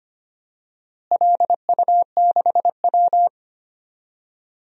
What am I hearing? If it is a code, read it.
Morse code, heard as a keyed tone, about 25 wpm, 700 Hz. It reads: LU6W
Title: DXCC-100 Call Signs - 4 Characters - Repeated 3x - 25wpm